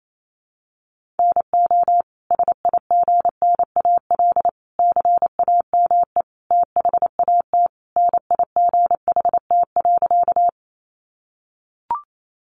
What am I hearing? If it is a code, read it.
Morse code, heard as a keyed tone, about 28 wpm, 700 Hz. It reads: NO SIGNAL CAME THAT NIGHT.
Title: Sentences from Top 1400 Words 28wpm